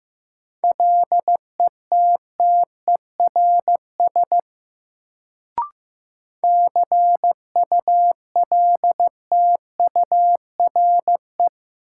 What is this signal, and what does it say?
Morse code, heard as a keyed tone, about 15 wpm, 700 Hz. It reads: LETTERS CULTURE
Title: Sentences from Top 1000 Words - Review Part 6 - New Word Drill - 15wpm